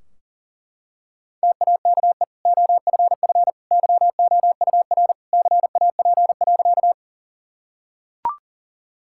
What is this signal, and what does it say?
Morse code, heard as a keyed tone, about 40 wpm, 700 Hz. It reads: TAKE OFF YOUR CAP.
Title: Sentences from Top 1600 Words 40wpm